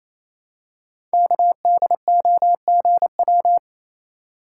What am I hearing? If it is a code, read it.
Morse code, heard as a keyed tone, about 28 wpm, 700 Hz. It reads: KDOGW
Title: Code Groups - 5 Letters - 28wpm